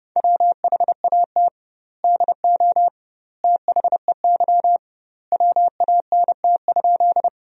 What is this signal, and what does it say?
Morse code, heard as a keyed tone, about 30 wpm, 700 Hz. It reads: WHAT DO THEY WANT?
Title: 2x Word-Spacing - Sentences from Top 200 Words - 30wpm